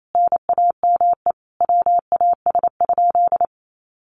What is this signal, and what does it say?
Morse code, heard as a keyed tone, about 28 wpm, 700 Hz. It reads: NAME WAS?
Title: Sentences from Top 200 Words 28wpm